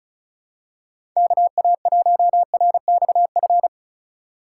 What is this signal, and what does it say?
Morse code, heard as a keyed tone, about 36 wpm, 700 Hz. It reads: KA1RXF
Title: USA Call Signs - 6 Characters - 2x Character Spacing - 35wpm